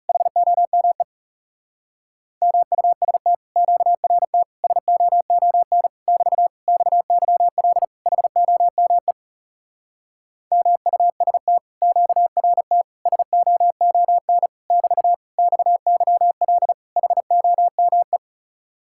Morse code, heard as a keyed tone, about 37 wpm, 700 Hz. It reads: HOME MUST QRT SOON <BT> XYL HOME MUST QRT SOON <BT> XYL HOME
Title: Speed-Racing Common QSO Phrases Encore - 30wpm